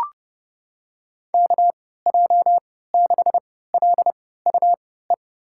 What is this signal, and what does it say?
Morse code, heard as a keyed tone, about 31 wpm, spaced as for 17 wpm, 700 Hz. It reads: KJ6LUE
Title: USA Call Signs - 3x Character Spacing - 30wpm